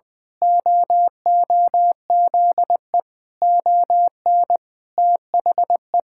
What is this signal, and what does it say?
Morse code, heard as a keyed tone, about 20 wpm, 700 Hz. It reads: OOZE ON THE
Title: Pangram Challenge v2 - 20wpm